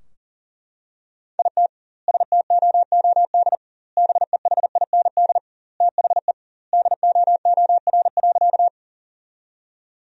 Morse code, heard as a keyed tone, about 41 wpm, 700 Hz. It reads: IT STOOD BEHIND THE DOOR.
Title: 2x Word-Spacing - Sentences from Top 500 Words - 40wpm